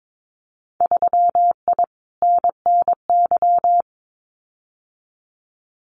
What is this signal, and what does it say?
Morse code, heard as a keyed tone, about 22 wpm, 700 Hz. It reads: 3I NNY
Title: Winter Field Day Exchanges 22wpm